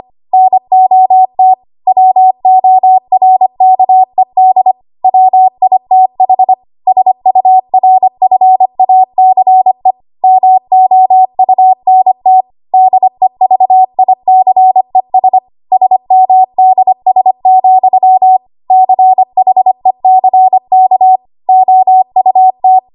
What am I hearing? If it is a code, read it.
Morse code, heard as a keyed tone, about 25 wpm, 750 Hz. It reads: NOT WORKED WITH SURFACE MOUNT DEVICES SMDS, CHECK OUT
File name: Archive / 25 WPM / 25WPM